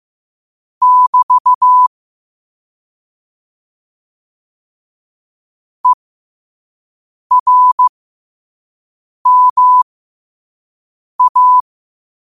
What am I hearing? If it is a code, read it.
Morse code, heard as a keyed tone, about 15 wpm, spaced as for 5 wpm, 1000 Hz. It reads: <BT> ERMA
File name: qotdfiles / 5wpm